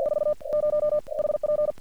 Morse code, heard as a keyed tone, about 50 wpm, spaced as for 37 wpm, 600 Hz. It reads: 40 HP
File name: archived / CRSnet